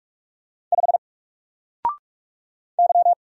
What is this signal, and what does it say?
Morse code, heard as a keyed tone, about 46 wpm, 700 Hz. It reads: 5 Y
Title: Single Letter-Number Alt 45wpm